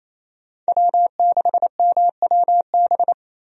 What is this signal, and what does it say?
Morse code, heard as a keyed tone, about 28 wpm, 700 Hz. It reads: W6MWB
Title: USA Call Signs - 5 Characters - 28wpm